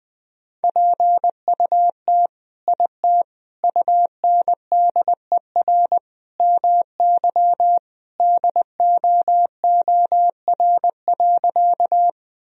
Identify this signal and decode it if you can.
Morse code, heard as a keyed tone, about 20 wpm, 700 Hz. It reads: PUT IT UNDER MY DOOR.
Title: Sentences from Top 1000 Words - Review Part 1 - 20wpm